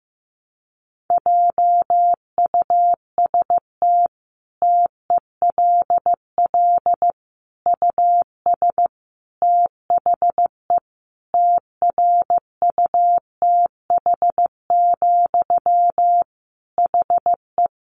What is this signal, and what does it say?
Morse code, heard as a keyed tone, about 15 wpm, 700 Hz. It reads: JUST TELL US THE TRUTH, HE
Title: Sentences from Top 800 Words 15wpm